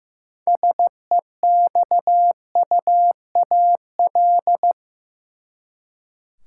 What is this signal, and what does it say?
Morse code, heard as a keyed tone, about 16 wpm, 700 Hz. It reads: SEXUAL
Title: Six Letter Words - 15wpm